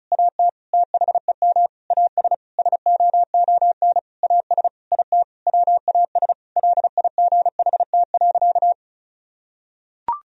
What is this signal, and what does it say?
Morse code, heard as a keyed tone, about 36 wpm, 700 Hz. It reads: AT THEM AS SOON AS IT WAS LIGHT.
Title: Sentences from Top 1000 Words - Review Part 1 - 35wpm